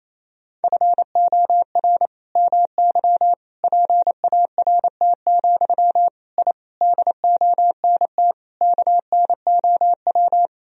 Morse code, heard as a keyed tone, about 28 wpm, 700 Hz. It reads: FOR MY PART, I DONT KNOW
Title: Sentences from Top 1000 Words - Review Part 1 - 28wpm